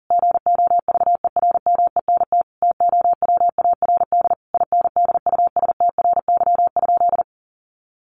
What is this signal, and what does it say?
Morse code, heard as a keyed tone, about 40 wpm, 700 Hz. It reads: GOVERMENT TOWARD INDUSTRY?
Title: Sentences from Top 1300 Words 40wpm